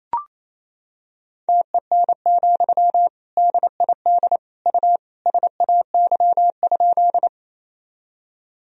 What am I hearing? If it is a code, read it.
Morse code, heard as a keyed tone, about 28 wpm, 700 Hz. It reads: TEN, DID U SAY?
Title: Sentences from Top 400 Words and Abbreviations 28wpm